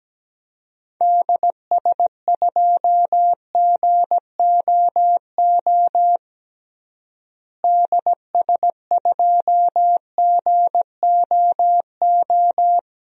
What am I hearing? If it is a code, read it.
Morse code, heard as a keyed tone, about 17 wpm, 700 Hz. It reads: DS2GOO DS2GOO
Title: DXCC-100 Call Signs - 6 Characters - Repeated 3x - 17wpm